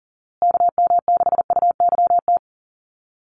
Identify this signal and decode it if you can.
Morse code, heard as a keyed tone, about 37 wpm, 700 Hz. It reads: KM6UYT
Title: USA Call Signs Repeated 2x 40wpm